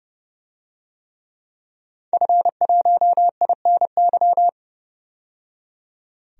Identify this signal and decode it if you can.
Morse code, heard as a keyed tone, about 30 wpm, 690 Hz. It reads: F1INY
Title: DXCC-100 Call Signs - 5 Characters - 30wpm